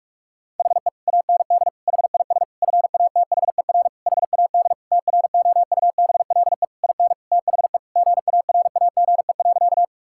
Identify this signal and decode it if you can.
Morse code, heard as a keyed tone, about 42 wpm, 700 Hz. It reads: HE AND HIS FATHER HAD TROUBLE IN THE GARAGE.
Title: Sentences from Top 1600 Words 45wpm